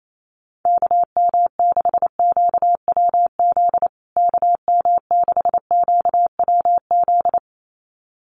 Morse code, heard as a keyed tone, about 28 wpm, 700 Hz. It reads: KM6QWZ KM6QWZ
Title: USA Call Signs Repeated 2x 28wpm